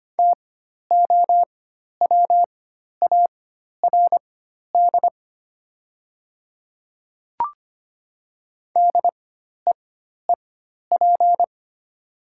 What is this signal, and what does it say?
Morse code, heard as a keyed tone, about 25 wpm, spaced as for 12 wpm, 700 Hz. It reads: TOWARD DEEP
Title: 4x Character Spacing - Top 500 Words - 25wpm